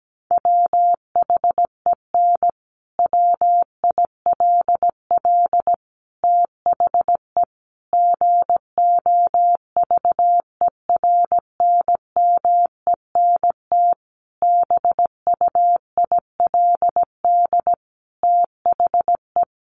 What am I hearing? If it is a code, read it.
Morse code, heard as a keyed tone, about 17 wpm, 700 Hz. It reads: WHEN WILL THE GOVERNMENT BUILD THE
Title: Sentences from Top 1400 Words 17wpm